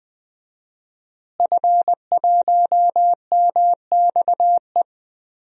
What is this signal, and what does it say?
Morse code, heard as a keyed tone, about 20 wpm, 700 Hz. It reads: F1MXE
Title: DXCC-100 Call Signs - 5 Characters - Repeated 3x - 20wpm